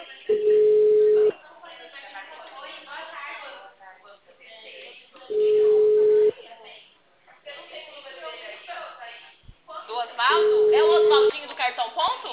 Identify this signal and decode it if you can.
Ringback tone (Europe, 425 Hz), with speech.
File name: out- -2013- - - .746